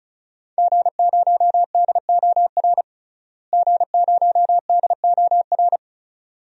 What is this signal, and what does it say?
Morse code, heard as a keyed tone, about 34 wpm, 700 Hz. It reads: G0DOR G0DOR
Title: DXCC-100 Call Signs - 5 Characters - Repeated 2x - 35wpm